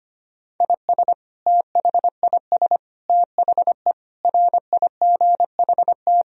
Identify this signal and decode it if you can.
Morse code, heard as a keyed tone, about 25 wpm, 700 Hz. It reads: IS THIS THE RIGHT
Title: Sentences from Top 1000 Words - Review Part 4 - 25wpm